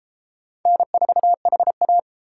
Morse code, heard as a keyed tone, about 29 wpm, 700 Hz. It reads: N4HA